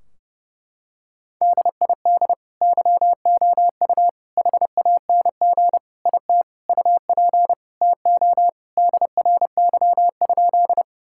Morse code, heard as a keyed tone, about 30 wpm, 700 Hz. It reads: DID YOU HANG IT UP TO DRY?
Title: Sentences from Top 1600 Words 30wpm